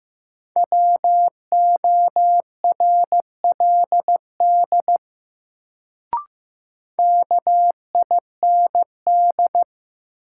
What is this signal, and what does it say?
Morse code, heard as a keyed tone, about 15 wpm, 700 Hz. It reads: WORLD KIND
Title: Sentences from Top 200 Words - New Word Drill - 15wpm